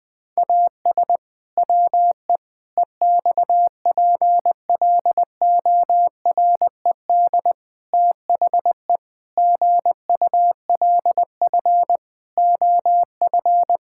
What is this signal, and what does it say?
Morse code, heard as a keyed tone, about 20 wpm, 700 Hz. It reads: AS WE EXPLORED THE GULF OF
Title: Pangram Challenge v2 - 20wpm